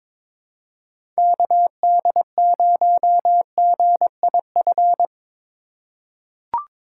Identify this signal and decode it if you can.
Morse code, heard as a keyed tone, about 22 wpm, 700 Hz. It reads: KD0GIF
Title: USA Call Signs - 6 Characters - 3x Character Spacing - 22wpm